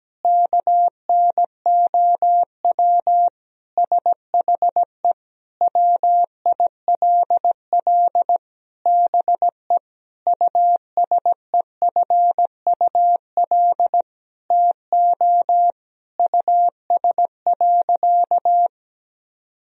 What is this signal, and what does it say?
Morse code, heard as a keyed tone, about 17 wpm, 700 Hz. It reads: KNOW SHE WILL BE USEFUL TO US.
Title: Sentences from Top 1000 Words - Review Part 4 - 17wpm